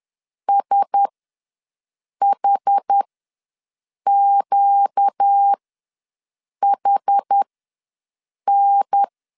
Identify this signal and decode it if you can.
Morse code, heard as a keyed tone, about 10 wpm, spaced as for 6 wpm, 800 Hz. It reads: SHQHN
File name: LEZIONI / 5 LBVC / 5 L ws 10wpm 3636